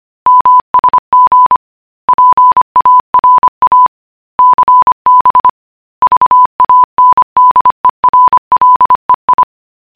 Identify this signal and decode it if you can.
Morse code, heard as a keyed tone, about 25 wpm, 1000 Hz. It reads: MSG PARA CB VANDERLEI